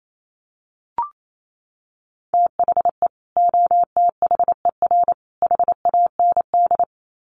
Morse code, heard as a keyed tone, about 28 wpm, 700 Hz. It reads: THE OTHER HAND
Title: Sets of 3 Words 28wpm